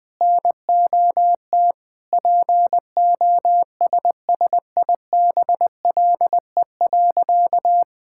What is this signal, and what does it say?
Morse code, heard as a keyed tone, about 19 wpm, 700 Hz. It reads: NOT POSSIBLE.
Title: Sentences from Top 1000 Words - Review Part 6 - 20wpm